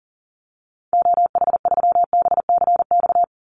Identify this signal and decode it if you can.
Morse code, heard as a keyed tone, about 40 wpm, 700 Hz. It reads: OH3BCX
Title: Call Signs Repeated 3x - 40wpm